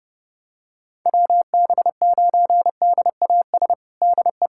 Morse code, heard as a keyed tone, about 31 wpm, 700 Hz. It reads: WB9DAS DE
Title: USA Callsign DE Callsign - 6 Characters - 30wpm